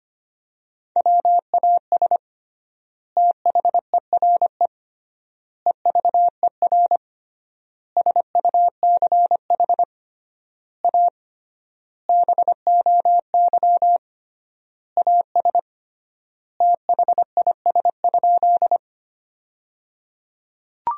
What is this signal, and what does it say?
Morse code, heard as a keyed tone, about 25 wpm, 700 Hz. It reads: WAS THERE EVER SUCH A BOY AS THIS?
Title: 3x Word-Spacing - Sentences from Top 400 Words - 25wpm